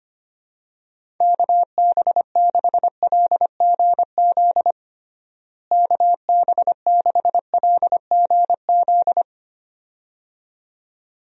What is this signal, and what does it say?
Morse code, heard as a keyed tone, about 25 wpm, 700 Hz. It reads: KB6LGZ KB6LGZ
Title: USA Call Signs - 6 Characters - Repeated 2x - 25wpm